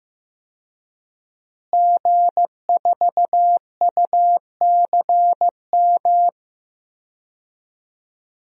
Morse code, heard as a keyed tone, about 15 wpm, 700 Hz. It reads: G4UCM
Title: DXCC-100 Call Signs - 5 Characters - Repeated 2x - 15wpm